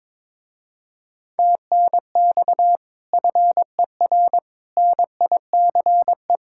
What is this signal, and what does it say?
Morse code, heard as a keyed tone, about 23 wpm, 700 Hz. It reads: TNX FER NICE
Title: Speed-Racing Common QSO Phrases - 22wpm